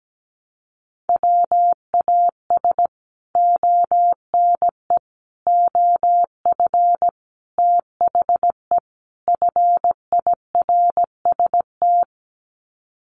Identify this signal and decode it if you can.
Morse code, heard as a keyed tone, about 17 wpm, 700 Hz. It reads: WAS ONE OF THE FIRST
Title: Sets of 5 Words 17wpm